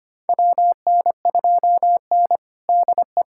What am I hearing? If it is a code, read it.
Morse code, heard as a keyed tone, about 25 wpm, 700 Hz. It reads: WN2N DE
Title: USA Callsign DE Callsign - 4 Characters - 25wpm